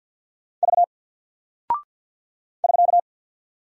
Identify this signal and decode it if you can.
Morse code, heard as a keyed tone, about 48 wpm, 700 Hz. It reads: V <SK>
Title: Single Character (Letter:Number:Punctuation:Prosign) Alt 50wpm